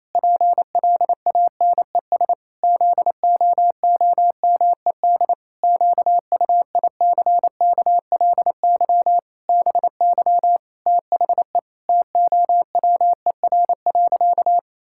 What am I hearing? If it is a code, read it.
Morse code, heard as a keyed tone, about 28 wpm, 700 Hz. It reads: PLANES ZOOMED QUICKLY BY THE TOWER.
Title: Pangram Challenge v2 - 28wpm